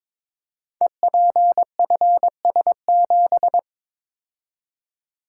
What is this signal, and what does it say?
Morse code, heard as a keyed tone, about 23 wpm, 690 Hz. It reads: EPFS7
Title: Code Groups - 5 Letter:Numbers - 22wpm